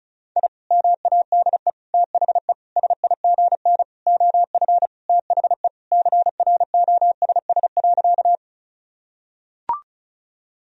Morse code, heard as a keyed tone, about 36 wpm, 700 Hz. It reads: I MADE THE SIGN OF THE CROSS.
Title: Sentences from Top 1000 Words - Review Part 1 - 35wpm